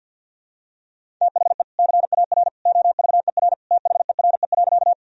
Morse code, heard as a keyed tone, about 53 wpm, 700 Hz. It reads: THE CAR OVER THERE.
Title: Sentences from Top 1000 Words - Review Part 1 - 50wpm